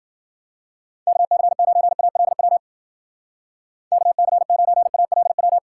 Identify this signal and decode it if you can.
Morse code, heard as a keyed tone, about 49 wpm, 700 Hz. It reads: KC9ALW KC9ALW
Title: Speed Racing - USA Call Signs 40wpm